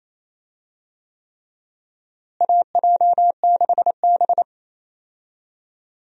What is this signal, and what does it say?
Morse code, heard as a keyed tone, about 28 wpm, 700 Hz. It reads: AJ6B
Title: USA Call Signs - 4 Characters - Repeated 3x - 28wpm